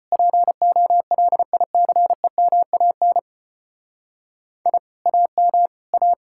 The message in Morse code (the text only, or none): POLICEMAN I AM A